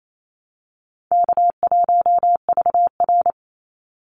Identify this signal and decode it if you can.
Morse code, heard as a keyed tone, about 28 wpm, 700 Hz. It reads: K1VR
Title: Easy Call Signs Repeated 2x - 28wpm